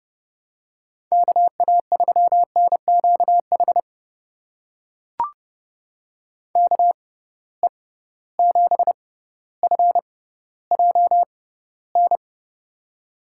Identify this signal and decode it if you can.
Morse code, heard as a keyed tone, about 30 wpm, spaced as for 15 wpm, 700 Hz. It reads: KA3NQH KE7FJN
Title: USA Call Signs - 6 Characters - 6x Character Spacing - 30wpm